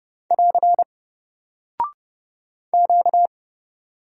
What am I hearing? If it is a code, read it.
Morse code, heard as a keyed tone, about 30 wpm, 700 Hz. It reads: <AR> Q
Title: Single Character (Letter:Number:Punctuation:Prosign) Alt 30wpm